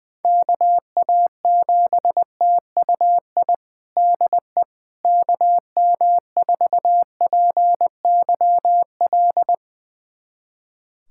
Morse code, heard as a keyed tone, about 20 wpm, 700 Hz. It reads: KA7TUI DE KM4PYL
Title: USA Callsign DE Callsign 20wpm